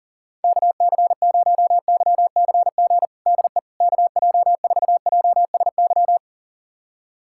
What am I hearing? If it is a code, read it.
Morse code, heard as a keyed tone, about 37 wpm, 700 Hz. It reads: KC0YCG DE KJ4JSY
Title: USA Callsign DE Callsign - 6 Characters - 40wpm